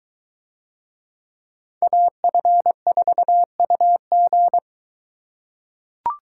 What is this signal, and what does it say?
Morse code, heard as a keyed tone, about 22 wpm, 700 Hz. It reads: AF4UG